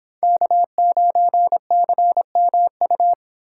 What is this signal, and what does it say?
Morse code, heard as a keyed tone, about 27 wpm, 700 Hz. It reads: K9CMU
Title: Speed Racing - USA Call Signs 22wpm